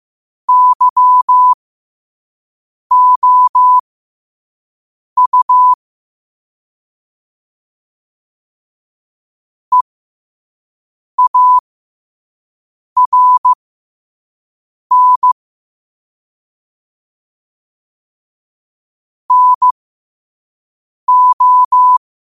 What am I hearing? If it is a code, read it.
Morse code, heard as a keyed tone, about 15 wpm, spaced as for 5 wpm, 1000 Hz. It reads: YOU EARN NO
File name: qotdfiles / 5wpm